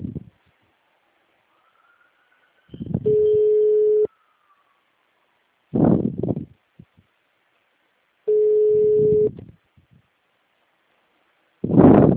Ringback tone (Europe, 425 Hz).